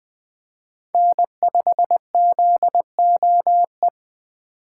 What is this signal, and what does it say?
Morse code, heard as a keyed tone, about 24 wpm, 700 Hz. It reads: N5ZOE
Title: Speed Racing - USA Call Signs 20wpm